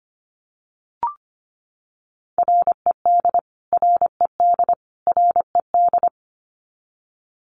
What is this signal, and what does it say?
Morse code, heard as a keyed tone, about 25 wpm, 700 Hz. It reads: RED RED RED
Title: Top 500 Words Repeated 25wpm